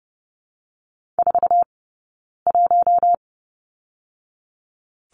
Morse code, heard as a keyed tone, about 30 wpm, 700 Hz. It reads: 4 1
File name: Lesson 20d - Number 3 - Two Characters 30wpm